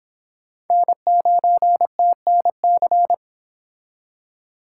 Morse code, heard as a keyed tone, about 30 wpm, 700 Hz. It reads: N9TNC